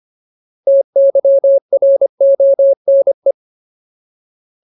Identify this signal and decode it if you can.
Morse code, heard as a keyed tone, about 25 wpm, 550 Hz. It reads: TYRONE
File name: Ultimate Word List / 25WPM Ultimate